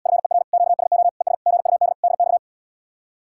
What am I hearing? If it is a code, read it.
Morse code, heard as a keyed tone, about 80 wpm, 700 Hz. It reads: FER QSO ES GUD DX